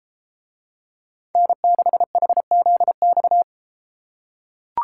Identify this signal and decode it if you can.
Morse code, heard as a keyed tone, about 33 wpm, 700 Hz. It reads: N6HZX